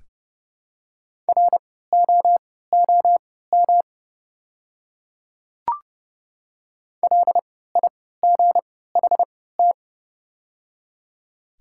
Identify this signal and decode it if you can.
Morse code, heard as a keyed tone, about 30 wpm, spaced as for 17 wpm, 700 Hz. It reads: ROOM LIGHT
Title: 3x Character Spacing - Top 300 Words - 30wpm